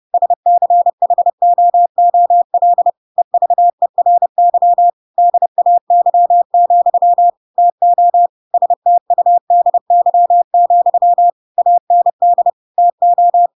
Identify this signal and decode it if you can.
Morse code, heard as a keyed tone, about 30 wpm, 700 Hz. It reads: SCHOOL EVERY DAY, TO STUDY, AND TO
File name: Pinocchio / Pinocchio ch8 30 30 ik1hge